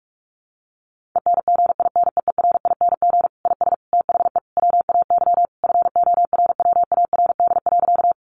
Morse code, heard as a keyed tone, about 44 wpm, 700 Hz. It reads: ENGINEERING IS THE WAY FORWARD.
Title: Sentences from Top 1500 Words 45wpm